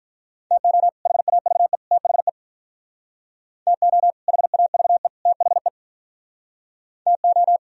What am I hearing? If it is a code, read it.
Morse code, heard as a keyed tone, about 44 wpm, 700 Hz. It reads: TO HAVE THE TO HAVE THE TO